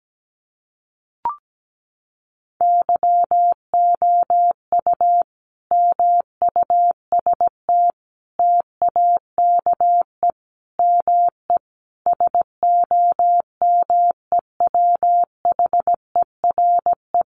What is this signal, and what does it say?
Morse code, heard as a keyed tone, about 17 wpm, 700 Hz. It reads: YOU MUST TAKE ME SOMEWHERE
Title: Sentences from Top 1400 Words 17wpm